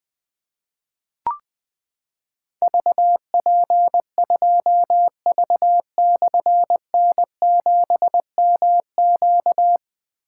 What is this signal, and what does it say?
Morse code, heard as a keyed tone, about 20 wpm, 700 Hz. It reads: VP2V/N7MQ
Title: Difficult Call Signs 20wpm